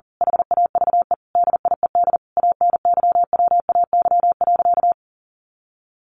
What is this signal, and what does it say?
Morse code, heard as a keyed tone, about 40 wpm, 700 Hz. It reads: HAVE DIED ANYWAY.
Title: A Sound Of Thunder 40wpm